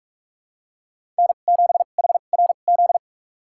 Morse code, heard as a keyed tone, about 46 wpm, spaced as for 33 wpm, 700 Hz. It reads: N7HRZ
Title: USA Call Signs - 2x Character Spacing - 45wpm